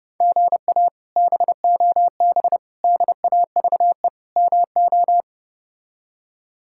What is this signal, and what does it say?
Morse code, heard as a keyed tone, about 32 wpm, 700 Hz. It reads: GA BOB DAVE MO
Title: K1USN SST QSO Practice - 30wpm